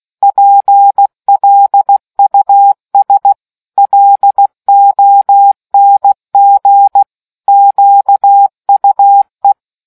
Morse code, heard as a keyed tone, about 16 wpm, 800 Hz. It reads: PLUS LONG QUE